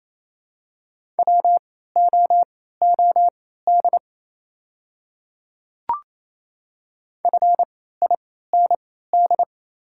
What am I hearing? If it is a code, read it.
Morse code, heard as a keyed tone, about 28 wpm, spaced as for 16 wpm, 700 Hz. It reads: WOOD FIND
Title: 3x Character Spacing - Top 300 Words - 28wpm